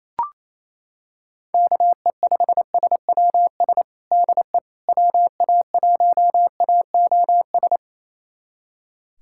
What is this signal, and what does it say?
Morse code, heard as a keyed tone, about 28 wpm, 700 Hz. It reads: KE5SWS DE WA1AOS
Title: USA Callsign DE Callsign - 6 Characters - 28wpm